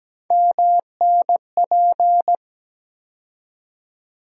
Morse code, heard as a keyed tone, about 17 wpm, 700 Hz. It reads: MNP
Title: USA Call Sign Postfixes - 3 Letters 17wpm